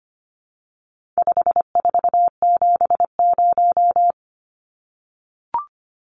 Morse code, heard as a keyed tone, about 25 wpm, 700 Hz. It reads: 5470
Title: Numbers - 4-Digit 25wpm